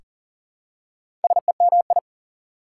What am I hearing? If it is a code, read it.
Morse code, heard as a keyed tone, about 40 wpm, 700 Hz. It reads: SEMI
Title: Prefixes and Suffixes 40wpm